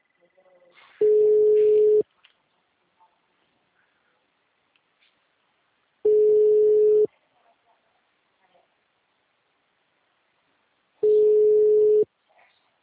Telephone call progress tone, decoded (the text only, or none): ringback tone